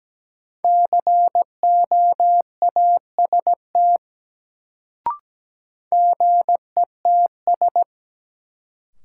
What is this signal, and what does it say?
Morse code, heard as a keyed tone, about 17 wpm, 700 Hz. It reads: COAST GETS
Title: Sentences from Top 1300 Words - New Word Drill Alt - 17wpm